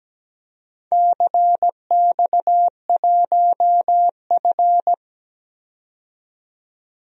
Morse code, heard as a keyed tone, about 17 wpm, 700 Hz. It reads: CX1F
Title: DXCC-100 Call Signs - 4 Characters - 17wpm